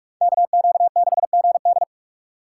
Morse code, heard as a keyed tone, about 46 wpm, 700 Hz. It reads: KQ6GD